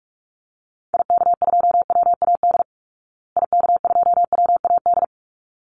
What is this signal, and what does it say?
Morse code, heard as a keyed tone, about 44 wpm, 700 Hz. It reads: IK2WAD IK2WAD
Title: Call Signs Repeated 3x - 45wpm